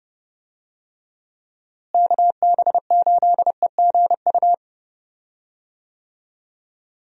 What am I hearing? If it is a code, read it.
Morse code, heard as a keyed tone, about 30 wpm, 700 Hz. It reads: KB8EGU
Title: USA Call Signs - 6 Characters - Repeated 3x - 30wpm